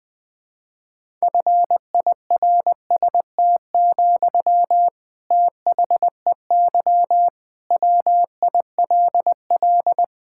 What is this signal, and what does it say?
Morse code, heard as a keyed tone, about 20 wpm, 700 Hz. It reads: FIRST, THEY WILL
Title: Sentences from Top 1000 Words - Review Part 6 - 20wpm